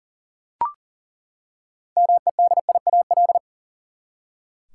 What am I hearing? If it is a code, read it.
Morse code, heard as a keyed tone, about 40 wpm, 700 Hz. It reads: MEDIAL
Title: K1USN SST QSO Practice 40wpm